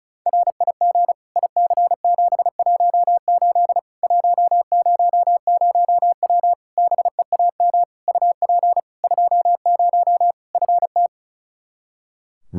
Morse code, heard as a keyed tone, about 36 wpm, 700 Hz. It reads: RIG IC718 100W BEAM UP 20 FT
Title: K1USN SST QSO Practice 35wpm